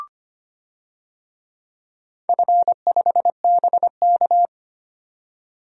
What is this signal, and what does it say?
Morse code, heard as a keyed tone, about 25 wpm, 700 Hz. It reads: F5BK
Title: DXCC-100 Call Signs - 4 Characters - 25wpm